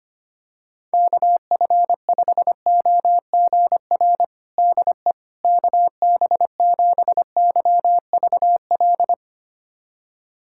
Morse code, heard as a keyed tone, about 25 wpm, 700 Hz. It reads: KF5OGR DE KB7YVL
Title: USA Callsign DE Callsign - 6 Characters - 25wpm